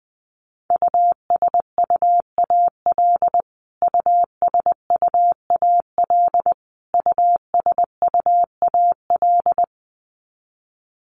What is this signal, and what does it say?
Morse code, heard as a keyed tone, about 20 wpm, 700 Hz. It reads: USUAL USUAL USUAL
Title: Top 500 Words Repeated 20wpm